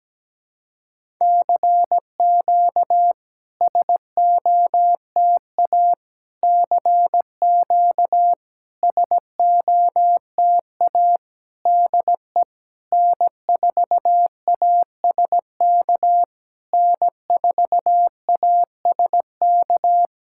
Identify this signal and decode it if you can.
Morse code, heard as a keyed tone, about 17 wpm, 700 Hz. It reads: CQ SOTA CQ SOTA DE N4ASK N4ASK
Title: SOTA QSO Practice 17wpm